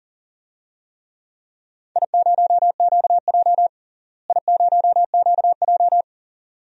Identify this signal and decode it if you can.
Morse code, heard as a keyed tone, about 40 wpm, 700 Hz. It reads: I0QJ I0QJ
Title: DXCC-100 Call Signs - 4 Characters - Repeated 2x - 40wpm